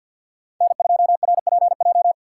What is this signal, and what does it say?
Morse code, heard as a keyed tone, about 52 wpm, 700 Hz. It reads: N2RPJ
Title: USA Call Signs - 5 Characters - 3x Character Spacing - 50wpm